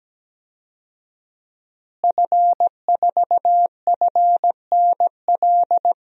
Morse code, heard as a keyed tone, about 17 wpm, 700 Hz. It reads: F4FNL
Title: DXCC-100 Call Signs - 5 Characters - Repeated 3x - 17wpm